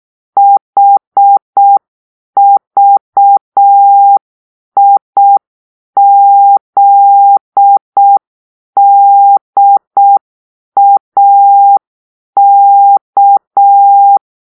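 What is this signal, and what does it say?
Morse code, heard as a keyed tone, about 6 wpm, 800 Hz. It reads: HVIZDAK
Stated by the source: Morse-code cipher made by a player team for the Kachna puzzle game.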